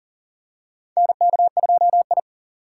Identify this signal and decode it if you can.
Morse code, heard as a keyed tone, about 40 wpm, 700 Hz. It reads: NK2I
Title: USA Call Signs - 4 Characters - 6x Character Spacing - 40wpm